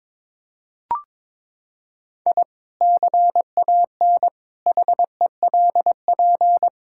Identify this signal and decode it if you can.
Morse code, heard as a keyed tone, about 21 wpm, 700 Hz. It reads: I CAN HELP
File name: practice / 22wpm / Sentences with 2-Word Phrases - 22wpm